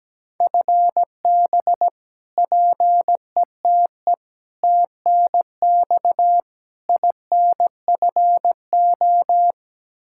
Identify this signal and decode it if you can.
Morse code, heard as a keyed tone, about 17 wpm, 700 Hz. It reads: FB PETE TNX INFO
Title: Practice QSO - Exchange 3 - More Info - 17wpm